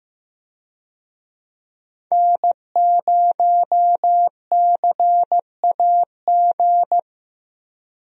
Morse code, heard as a keyed tone, about 15 wpm, 700 Hz. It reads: N0CAG
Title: USA Call Signs - 5 Characters - 15wpm